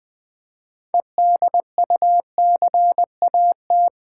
Morse code, heard as a keyed tone, about 20 wpm, 700 Hz. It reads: EDUCAT
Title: Build A Word - 2 Letters At A Time - 8-Letter Words - 20wpm